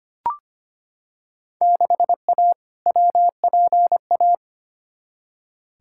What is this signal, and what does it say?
Morse code, heard as a keyed tone, about 25 wpm, 700 Hz. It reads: 6A WPA
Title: Field Day Exchanges Encore - 25wpm